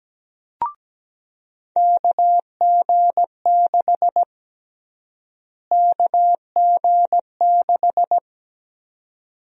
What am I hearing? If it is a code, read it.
Morse code, heard as a keyed tone, about 17 wpm, 700 Hz. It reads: KG6 KG6
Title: USA Call Sign Prefixes - Repeated 2x - 17wpm